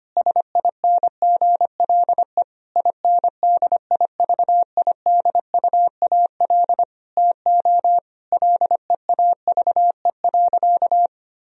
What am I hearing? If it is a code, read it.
Morse code, heard as a keyed tone, about 25 wpm, 700 Hz. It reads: SINGLE INDIVIDUAL TO LEAVE.
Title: Sentences from Top 1000 Words - Review Part 4 - 25wpm